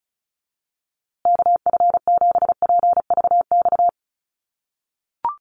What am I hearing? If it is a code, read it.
Morse code, heard as a keyed tone, about 34 wpm, 700 Hz. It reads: KF7PVX
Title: USA Call Signs Repeated 3x 35wpm